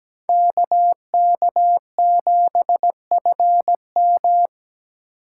Morse code, heard as a keyed tone, about 17 wpm, 700 Hz. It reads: KK7FM
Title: USA Call Signs - 5 Characters - 4x Character Spacing - 17wpm